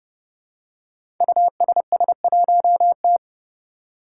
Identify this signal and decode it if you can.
Morse code, heard as a keyed tone, about 30 wpm, 700 Hz. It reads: USS1T
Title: Five Challenging Letter-Number Combinations 30wpm